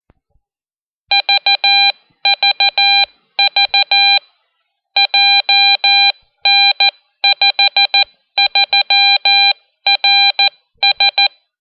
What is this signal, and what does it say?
Morse code, heard as a keyed tone, about 14 wpm, 2400 Hz, with noise. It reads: VVV JN53RS
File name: Identificativo-CW-Beacon